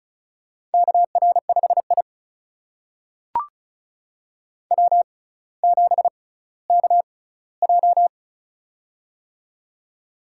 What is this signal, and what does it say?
Morse code, heard as a keyed tone, about 36 wpm, spaced as for 18 wpm, 700 Hz. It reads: KR5I W7KJ
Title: USA Call Signs - 4 Characters - 6x Character Spacing - 35wpm